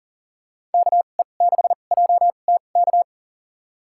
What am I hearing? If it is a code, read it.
Morse code, heard as a keyed tone, about 40 wpm, spaced as for 29 wpm, 700 Hz. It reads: KE6JTK
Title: USA Call Signs - 2x Character Spacing - 40wpm